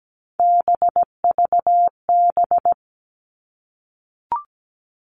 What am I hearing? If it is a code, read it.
Morse code, heard as a keyed tone, about 17 wpm, 700 Hz. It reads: BVB
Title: All Cut Numbers - 3-Digit 17wpm